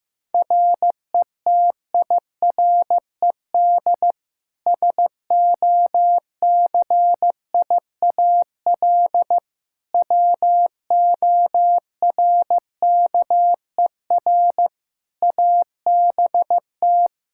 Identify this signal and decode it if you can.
Morse code, heard as a keyed tone, about 15 wpm, 700 Hz. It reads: RETIRED SOCIAL WORKER ABT
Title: Practice QSO - Exchange 3 - More Info - 15wpm